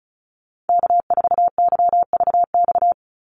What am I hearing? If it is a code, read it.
Morse code, heard as a keyed tone, about 34 wpm, 700 Hz. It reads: K4YVX
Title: USA Call Signs 35wpm